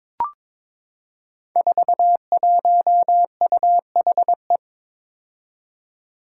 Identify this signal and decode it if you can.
Morse code, heard as a keyed tone, about 22 wpm, 700 Hz. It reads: 41UHE
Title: Five Challenging Letter-Number Combinations 22wpm